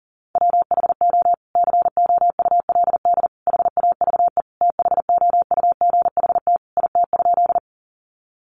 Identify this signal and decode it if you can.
Morse code, heard as a keyed tone, about 40 wpm, 700 Hz. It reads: WHO COULD HAVE THOUGHT IT?
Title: Sentences from Top 300 Words 40wpm